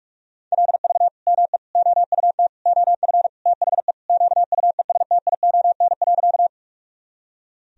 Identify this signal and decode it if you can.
Morse code, heard as a keyed tone, about 47 wpm, 700 Hz. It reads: LV ME OUT OF THE QUESTION.
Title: Sentences from Top 400 Words and Abbreviations 45wpm